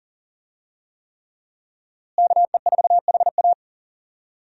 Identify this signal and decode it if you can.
Morse code, heard as a keyed tone, about 40 wpm, 700 Hz. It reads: KE4HA
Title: USA Call Signs - 5 Characters - Repeated 3x - 40wpm